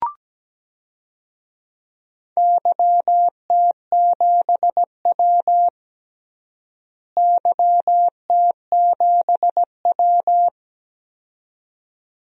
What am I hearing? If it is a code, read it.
Morse code, heard as a keyed tone, about 17 wpm, 700 Hz. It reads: YT7W YT7W
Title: DXCC-100 Call Signs - 4 Characters - Repeated 2x - 17wpm